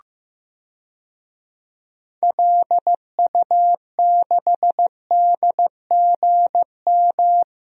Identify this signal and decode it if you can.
Morse code, heard as a keyed tone, about 15 wpm, 700 Hz. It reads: LU6DGM
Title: DXCC-100 Call Signs - 6 Characters - 15wpm